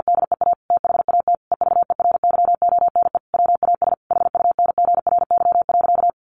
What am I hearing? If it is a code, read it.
Morse code, heard as a keyed tone, about 49 wpm, 700 Hz. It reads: DEA THAT EVERYONE WAS HUNGRY.
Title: Sentences from Top 1400 Words 50wpm